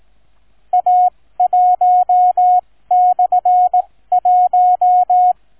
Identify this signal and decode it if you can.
Morse code, heard as a keyed tone, about 18 wpm, spaced as for 15 wpm, 730 Hz. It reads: A1/1